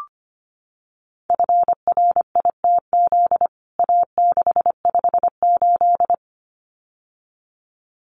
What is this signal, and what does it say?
Morse code, heard as a keyed tone, about 25 wpm, 700 Hz. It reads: FRITZ A658
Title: CWT Exchange 25wpm